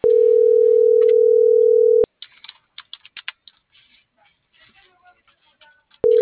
Ringback tone (North America), 440 + 480 Hz.